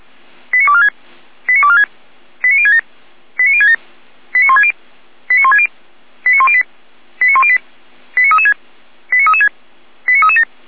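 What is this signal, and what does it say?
Five-tone selective call (ZVEI), heard as quick runs of five tones, tones of about 70 ms. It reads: 89246 89246 89067 89067 89160 89160 89108 89108 89336 89336 89337
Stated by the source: ZVEI five-tone selective call 89150 received with an SDR.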